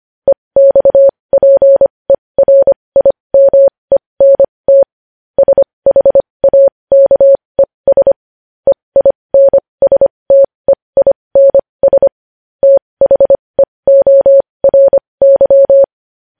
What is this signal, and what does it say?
Morse code, heard as a keyed tone, about 25 wpm, spaced as for 20 wpm, 550 Hz. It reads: EXPERIMENT SHAKES EINSTEINS THEORY
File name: podcast / News headlines-1x-25x20